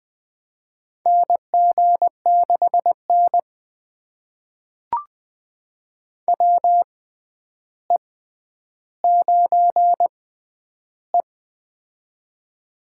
Morse code, heard as a keyed tone, about 20 wpm, spaced as for 10 wpm, 700 Hz. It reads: NG6N WE9E
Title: USA Call Signs - 4 Characters - 6x Character Spacing - 20wpm